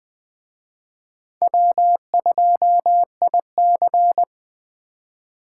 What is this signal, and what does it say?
Morse code, heard as a keyed tone, about 20 wpm, 700 Hz. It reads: W2IC
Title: Speed Racing - USA Call Signs 17wpm